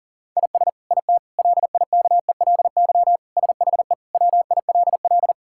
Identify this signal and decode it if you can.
Morse code, heard as a keyed tone, about 40 wpm, 700 Hz. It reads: IS IT LIKELY SHE WILL
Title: Sentences from Top 1000 Words - Review Part 5 - 40wpm